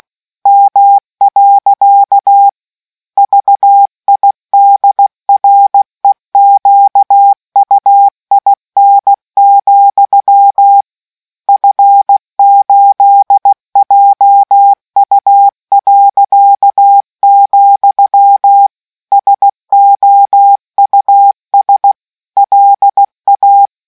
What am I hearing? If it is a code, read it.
Morse code, heard as a keyed tone, about 16 wpm, 800 Hz. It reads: M. VIDREQUIN, F8JU., SOUS LA